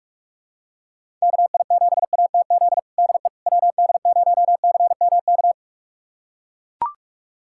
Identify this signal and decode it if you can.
Morse code, heard as a keyed tone, about 44 wpm, 700 Hz. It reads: KI7ATZ DE WD0CMK
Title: USA Callsign DE Callsign - 6 Characters - 45wpm